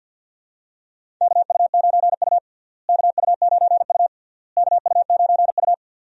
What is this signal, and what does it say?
Morse code, heard as a keyed tone, about 47 wpm, 700 Hz. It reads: KU9U KU9U KU9U
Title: USA Call Signs - 4 Characters - Repeated 3x - 50wpm